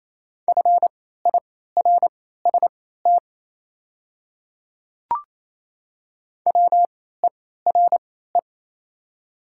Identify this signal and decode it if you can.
Morse code, heard as a keyed tone, about 27 wpm, spaced as for 16 wpm, 690 Hz. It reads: FIRST WERE
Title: 3x Character Spacing - Top 100 Words - 28wpm